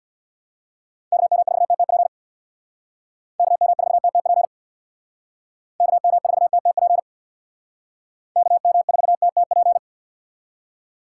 Morse code, heard as a keyed tone, about 65 wpm, 700 Hz. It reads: KM4TTP KM4TTP KM4TTP KM4TTP